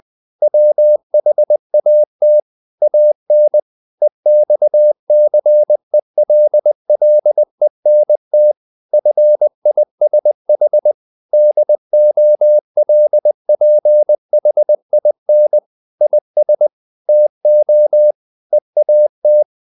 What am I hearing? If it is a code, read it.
Morse code, heard as a keyed tone, about 20 wpm, 600 Hz. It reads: WHAT AN EXCELLENT FISH DOLPHIN IS TO EAT
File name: the old man and the sea part 2 20wpm